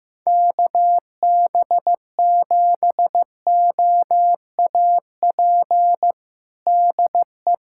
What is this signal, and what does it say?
Morse code, heard as a keyed tone, about 16 wpm, 700 Hz. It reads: KB7OAP DE
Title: USA Callsign DE Callsign - 6 Characters - 15wpm